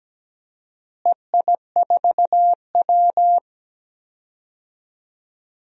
Morse code, heard as a keyed tone, about 17 wpm, 700 Hz. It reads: EI4W
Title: DXCC-100 Call Signs - 4 Characters - Repeated 3x - 17wpm